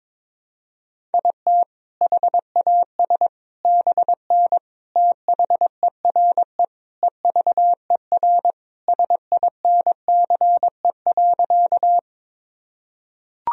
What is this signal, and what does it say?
Morse code, heard as a keyed tone, about 21 wpm, 700 Hz. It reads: IT HAS BN THERE EVER SINCE.
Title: Sentences from Top 400 Words and Abbreviations 22wpm